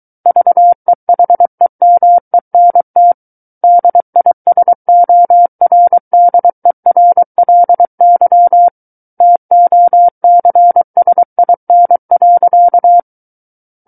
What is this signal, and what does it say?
Morse code, heard as a keyed tone, about 23 wpm, 700 Hz. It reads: VEHEMENT DISORDERLY TOCSIN.
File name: Book One / 14-Chapter 14 - 23 WPM